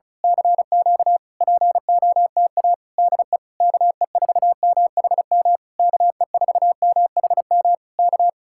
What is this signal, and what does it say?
Morse code, heard as a keyed tone, about 32 wpm, 700 Hz. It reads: CQ POTA DE KE4MHM KE4MHM K
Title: Speed-Racing Common QSO Phrases Encore - 35wpm